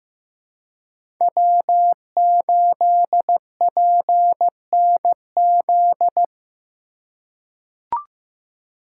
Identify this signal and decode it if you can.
Morse code, heard as a keyed tone, about 15 wpm, 700 Hz. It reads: W8PNZ
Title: USA Call Signs - 5 Characters - 2x Character Spacing - 15wpm